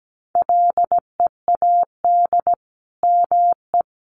Morse code, heard as a keyed tone, about 17 wpm, 700 Hz. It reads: LEAD ME
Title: Sentences from Top 500 Words 17wpm